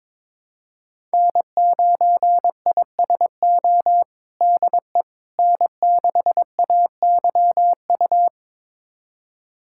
Morse code, heard as a keyed tone, about 22 wpm, 700 Hz. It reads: N9ISO DE N6AYU
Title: USA Callsign DE Callsign - 5 Characters - 22wpm